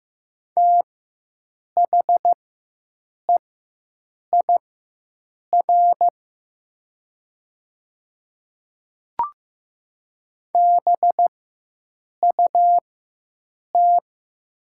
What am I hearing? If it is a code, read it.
Morse code, heard as a keyed tone, about 15 wpm, spaced as for 7 wpm, 700 Hz. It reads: THEIR BUT
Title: 4x Character Spacing - Top 100 Words - 15wpm